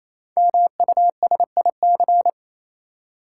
Morse code, heard as a keyed tone, about 27 wpm, 690 Hz. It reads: MUSIC
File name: Easy Words - Speed-Racing with Hints - 22wpm